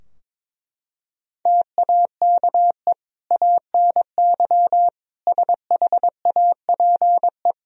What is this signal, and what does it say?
Morse code, heard as a keyed tone, about 22 wpm, 700 Hz. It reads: TAKE ANY SHAPE
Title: Sentences from Top 1000 Words - Review Part 3 - 22wpm